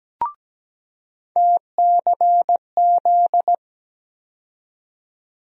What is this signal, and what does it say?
Morse code, heard as a keyed tone, about 17 wpm, 700 Hz. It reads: TCZ
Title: USA Call Sign Postfixes - 3 Letters 17wpm